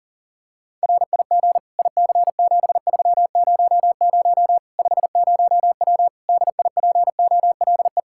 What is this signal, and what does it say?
Morse code, heard as a keyed tone, about 40 wpm, 700 Hz. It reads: RIG IC7300 50W DIPOLE
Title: Practice QSO - Exchange 2 - Info - 40wpm